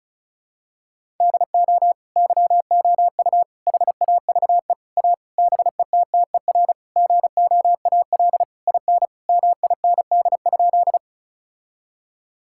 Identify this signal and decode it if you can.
Morse code, heard as a keyed tone, about 36 wpm, 700 Hz. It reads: DO YOU HAVE A BETTER GOAL IN MIND?
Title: Sentences from Top 1000 Words - Review Part 5 - 35wpm